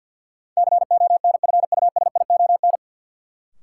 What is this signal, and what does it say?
Morse code, heard as a keyed tone, about 52 wpm, 700 Hz. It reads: CONFUSION
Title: Sentences from Top 1600 Words - New Word Drill - 50wpm